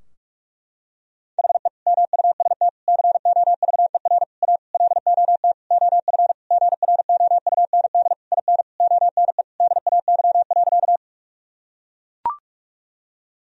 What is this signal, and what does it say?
Morse code, heard as a keyed tone, about 41 wpm, 700 Hz. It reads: HE MUST COVER A LOT OF GROUND IN ONE DAY.
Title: Sentences from Top 1000 Words - Review Part 3 - 45wpm